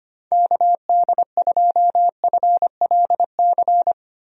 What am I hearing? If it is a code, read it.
Morse code, heard as a keyed tone, about 25 wpm, 700 Hz. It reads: KD2FLC
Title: USA Call Signs - 6 Characters - Repeated 2x - 25wpm